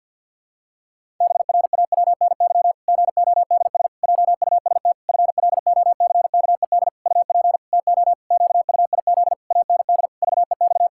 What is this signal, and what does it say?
Morse code, heard as a keyed tone, about 52 wpm, 700 Hz. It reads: BRAWNY GODS JUST FLOCKED UP TO QUIZ AND VEX
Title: Pangram Challenge v2 - 50wpm